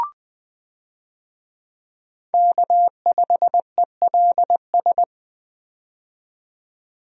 Morse code, heard as a keyed tone, about 20 wpm, 700 Hz. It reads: K5ELS
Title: USA Call Signs - 5 Characters - 20wpm